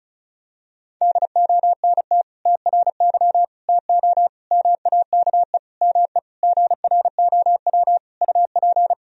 Morse code, heard as a keyed tone, about 36 wpm, 700 Hz. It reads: DONT TRY TO MAKE ME GROW UP
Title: Sentences from Top 1000 Words - Review Part 1 - 35wpm